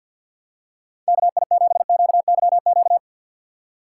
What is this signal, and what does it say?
Morse code, heard as a keyed tone, about 53 wpm, 700 Hz. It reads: KI7QYQ
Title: USA Call Signs - 6 Characters - 4x Character Spacing - 50wpm